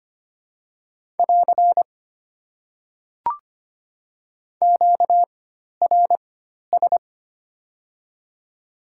Morse code, heard as a keyed tone, about 25 wpm, spaced as for 12 wpm, 690 Hz. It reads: <AR> QRS
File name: Top 90 QSO Elements - 4x Character Spacing - 25wpm